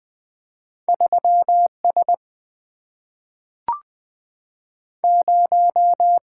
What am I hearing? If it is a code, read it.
Morse code, heard as a keyed tone, about 20 wpm, 700 Hz. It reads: 3S 0
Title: Single-Letter Single-Number 20wpm